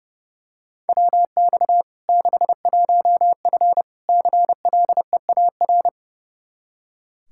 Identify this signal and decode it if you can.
Morse code, heard as a keyed tone, about 32 wpm, 700 Hz. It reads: WX 61F CLEAR
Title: K1USN SST QSO Practice 30wpm